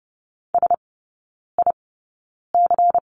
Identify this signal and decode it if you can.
Morse code, heard as a keyed tone, about 30 wpm, spaced as for 9 wpm, 700 Hz. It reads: SIC
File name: Lesson 16e - Letter c - Three Characters 30wpm